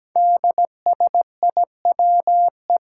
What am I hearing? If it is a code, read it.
Morse code, heard as a keyed tone, about 17 wpm, 700 Hz. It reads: DSIWE
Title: Five Challenging Letter-Number Combinations 17wpm